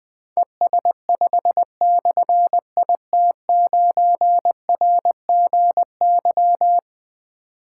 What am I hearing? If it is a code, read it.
Morse code, heard as a keyed tone, about 20 wpm, 700 Hz. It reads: ES5/IT9RGY
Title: Difficult Call Signs 20wpm